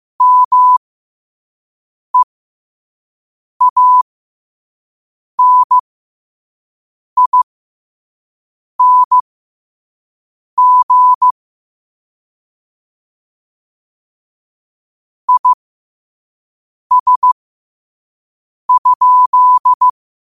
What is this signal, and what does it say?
Morse code, heard as a keyed tone, about 15 wpm, spaced as for 5 wpm, 1000 Hz. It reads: MEANING IS?
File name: 5wpm